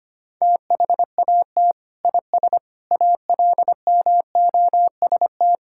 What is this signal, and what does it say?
Morse code, heard as a keyed tone, about 25 wpm, 700 Hz. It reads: THAT IS ALMOST
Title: Sentences from Top 1600 Words 25wpm